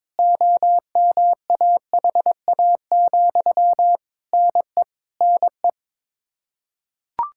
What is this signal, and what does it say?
Morse code, heard as a keyed tone, about 22 wpm, 700 Hz. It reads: OMAHA, NE NE
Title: Practice QSO - Exchange 1 - Beginning - 22wpm